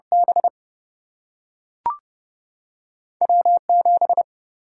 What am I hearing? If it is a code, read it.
Morse code, heard as a keyed tone, about 30 wpm, 700 Hz. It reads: B W7
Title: Single-Letter Single-Number 30wpm